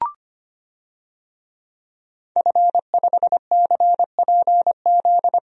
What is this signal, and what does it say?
Morse code, heard as a keyed tone, about 25 wpm, 700 Hz. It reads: F5CPZ
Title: DXCC-100 Call Signs - 5 Characters - 25wpm